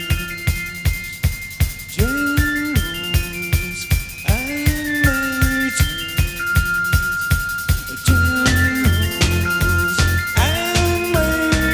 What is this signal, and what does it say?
Dial-up modem handshake.